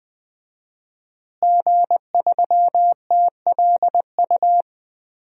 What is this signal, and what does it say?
Morse code, heard as a keyed tone, about 20 wpm, 700 Hz. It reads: G3TLU
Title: DXCC-100 Call Signs - 5 Characters - Repeated 3x - 20wpm